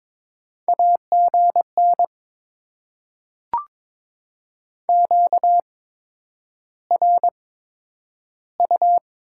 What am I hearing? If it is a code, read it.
Morse code, heard as a keyed tone, about 22 wpm, spaced as for 9 wpm, 700 Hz. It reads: AGN QRU
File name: Top 90 QSO Elements - 8x Character Spacing - 22wpm